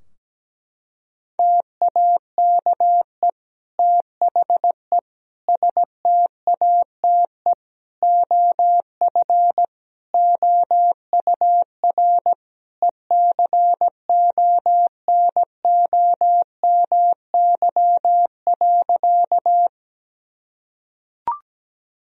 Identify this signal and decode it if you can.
Morse code, heard as a keyed tone, about 17 wpm, 700 Hz. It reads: TAKE THE STATE OF OUR ECONOMY.
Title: Sentences from Top 1000 Words - Review Part 3 - 17wpm